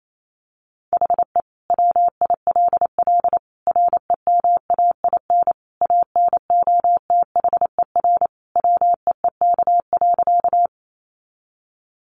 Morse code, heard as a keyed tone, about 28 wpm, 700 Hz. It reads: HE WILL REMAIN ANOTHER WEEK.
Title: Sentences from Top 1200 Words 28wpm